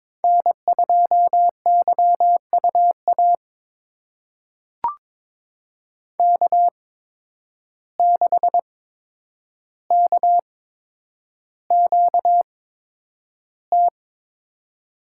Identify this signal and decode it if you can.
Morse code, heard as a keyed tone, about 22 wpm, spaced as for 9 wpm, 700 Hz. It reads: N2YUA K6KQT
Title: USA Call Signs - 5 Characters - 8x Character Spacing - 22wpm